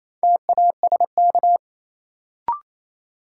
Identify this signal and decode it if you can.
Morse code, heard as a keyed tone, about 28 wpm, 700 Hz. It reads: TASK
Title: Sentences from Top 1000 Words - Review Part 7 - New Word Drill - 28wpm